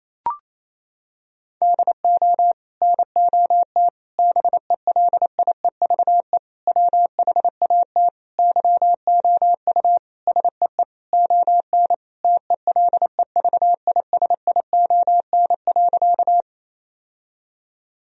Morse code, heard as a keyed tone, about 28 wpm, 700 Hz. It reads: DO NOT BELIEVE WHAT YOU SEE ON TELEVISION.
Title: Sentences from Top 1000 Words - Review Part 4 - 28wpm